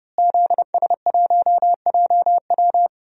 Morse code, heard as a keyed tone, about 29 wpm, 700 Hz. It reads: ZS1JW
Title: DXCC-100 Call Signs - 5 Characters - Repeated 3x - 30wpm